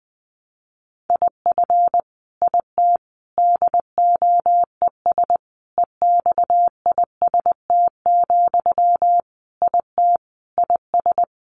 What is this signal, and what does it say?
Morse code, heard as a keyed tone, about 20 wpm, 700 Hz. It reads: IF IT DOES EXIST, IT IS
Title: Sentences from Top 1300 Words 20wpm